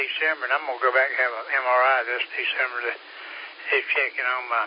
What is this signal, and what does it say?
Emergency Alert System (SAME) tones, heard as harsh buzzing bursts, with speech. It is heard from 1.0 to 4.3 s.